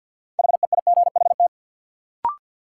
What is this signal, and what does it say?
Morse code, heard as a keyed tone, about 50 wpm, 700 Hz. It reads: HEIGHT
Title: Sentences from Top 1600 Words - New Word Drill Alt - 50wpm